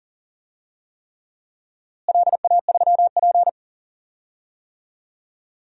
Morse code, heard as a keyed tone, about 40 wpm, 700 Hz. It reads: LA3P
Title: DXCC-100 Call Signs - 4 Characters - Repeated 3x - 40wpm